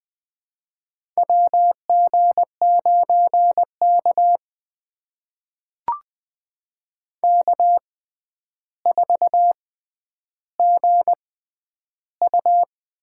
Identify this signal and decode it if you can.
Morse code, heard as a keyed tone, about 20 wpm, spaced as for 10 wpm, 700 Hz. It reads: WG9K K4GU
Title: USA Call Signs - 4 Characters - 6x Character Spacing - 20wpm